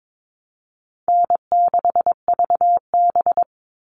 Morse code, heard as a keyed tone, about 22 wpm, 700 Hz. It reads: N6VB
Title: All Cut Numbers - 4-Digit 22wpm